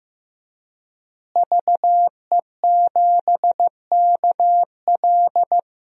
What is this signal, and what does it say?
Morse code, heard as a keyed tone, about 15 wpm, 700 Hz. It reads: VE7KL
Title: DXCC-100 Call Signs - 5 Characters - Repeated 3x - 15wpm